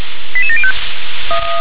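Five-tone selective call (ZVEI), heard as quick runs of five tones, tones of about 70 ms. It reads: 88705
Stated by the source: ZVEI five-tone selective call 88775 received on SDR.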